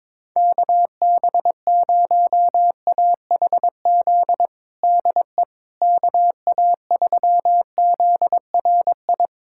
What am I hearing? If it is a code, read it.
Morse code, heard as a keyed tone, about 21 wpm, 700 Hz. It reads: KB0AHZ DE KA3ZRI
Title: USA Callsign DE Callsign - 6 Characters - 22wpm